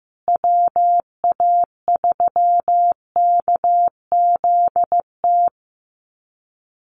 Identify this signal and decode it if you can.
Morse code, heard as a keyed tone, about 15 wpm, 700 Hz. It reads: WA3KZT
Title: USA Call Signs Repeated 3x 15wpm